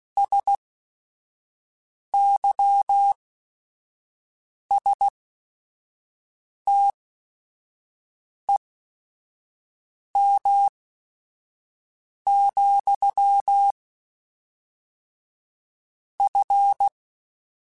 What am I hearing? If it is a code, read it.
Morse code, heard as a keyed tone, about 16 wpm, spaced as for 5 wpm, 800 Hz. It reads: SYSTEM, F
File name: Technical Subjects / Technical Subjects-mp3s / Sw Pwr Sup @5